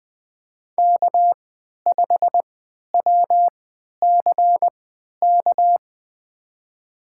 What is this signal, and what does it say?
Morse code, heard as a keyed tone, about 20 wpm, spaced as for 11 wpm, 700 Hz. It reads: K5WCK
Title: USA Call Signs - 5 Characters - 3x Character Spacing - 20wpm